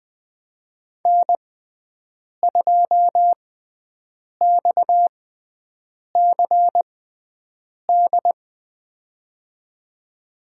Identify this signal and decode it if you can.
Morse code, heard as a keyed tone, about 20 wpm, spaced as for 7 wpm, 700 Hz. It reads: N2XCD
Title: USA Call Signs - 5 Characters - 6x Character Spacing - 20wpm